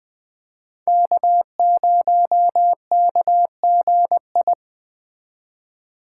Morse code, heard as a keyed tone, about 21 wpm, 700 Hz. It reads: K0KGI